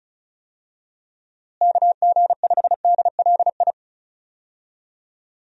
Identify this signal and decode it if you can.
Morse code, heard as a keyed tone, about 34 wpm, 700 Hz. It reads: KG5DLI
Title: USA Call Signs - 6 Characters - Repeated 3x - 35wpm